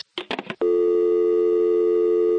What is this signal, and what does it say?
Dial tone (North America), 350 + 440 Hz.
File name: phone numbers / calls / toluca security var1b